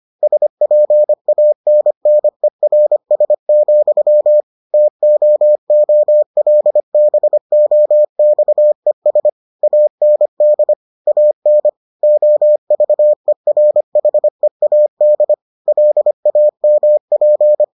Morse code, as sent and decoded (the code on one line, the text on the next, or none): ... .--. .- -. -. .-. ... --..-- / - --- --- .-.. -... --- -..- . ... / .- -. -.. / .- -. / --- ...- . .-. .... .- -.. / .-.. .- -- .--.
SPANNERS, TOOLBOXES AND AN OVERHEAD LAMP